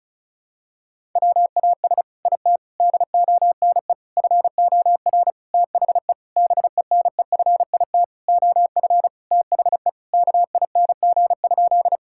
Morse code, heard as a keyed tone, about 36 wpm, 700 Hz. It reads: WAS IT DONE FOR THE BENEFIT OF THE KING?
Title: Sentences from Top 1000 Words - Review Part 6 - 35wpm